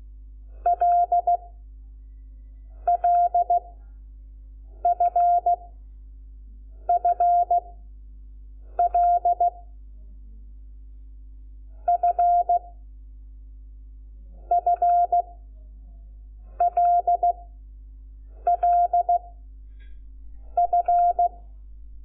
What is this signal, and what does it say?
Morse code, heard as a keyed tone, about 16 wpm, spaced as for 6 wpm, 700 Hz. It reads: LLFFL F FLLF
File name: Lecciones / CLASE Nº5